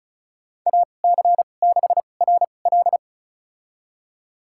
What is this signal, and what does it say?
Morse code, heard as a keyed tone, about 36 wpm, spaced as for 25 wpm, 700 Hz. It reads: AC6RL
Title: USA Call Signs - 5 Characters - 2x Character Spacing - 35wpm